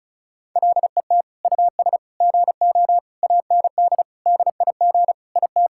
Morse code, heard as a keyed tone, about 36 wpm, 700 Hz. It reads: LET US GO AND DIG IT
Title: Sentences from Top 1600 Words 35wpm